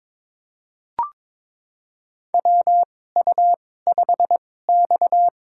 Morse code, heard as a keyed tone, about 21 wpm, spaced as for 16 wpm, 700 Hz. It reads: WU5X
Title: USA Call Signs - 2x Character Spacing - 22wpm